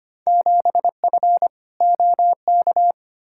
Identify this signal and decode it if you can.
Morse code, heard as a keyed tone, about 25 wpm, 700 Hz. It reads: 7F OK
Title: Field Day Exchanges Encore - 25wpm